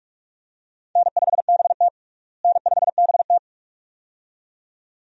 Morse code, heard as a keyed tone, about 47 wpm, 700 Hz. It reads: N5BT N5BT
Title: USA Call Signs - 4 Characters - Repeated 2x - 45wpm